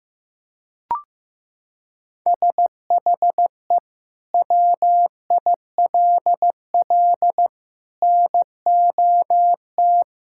Morse code, heard as a keyed tone, about 15 wpm, 700 Hz. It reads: SHE WILL NOT
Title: Sentences from Top 1000 Words - Review Part 6 - 15wpm